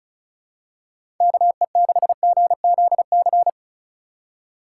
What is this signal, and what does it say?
Morse code, heard as a keyed tone, about 36 wpm, 700 Hz. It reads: KE6GZC
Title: USA Call Signs - 6 Characters - 3x Character Spacing - 35wpm